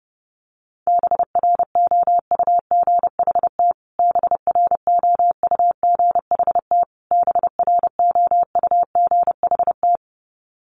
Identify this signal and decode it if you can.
Morse code, heard as a keyed tone, about 30 wpm, 700 Hz. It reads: BROUGHT BROUGHT BROUGHT
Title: Top 500 Words Repeated 30wpm